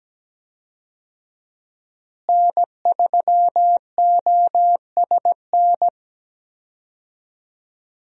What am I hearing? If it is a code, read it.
Morse code, heard as a keyed tone, about 17 wpm, 700 Hz. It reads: N3OSN
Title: USA Call Signs - 5 Characters - Repeated 2x - 17wpm